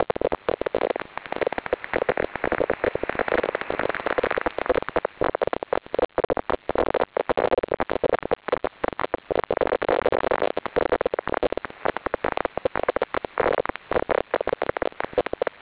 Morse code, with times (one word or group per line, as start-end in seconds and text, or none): none